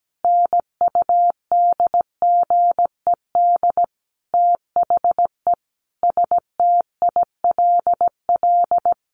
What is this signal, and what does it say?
Morse code, heard as a keyed tone, about 17 wpm, 700 Hz. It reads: NUDGED THE STILL
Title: A Sound Of Thunder 17wpm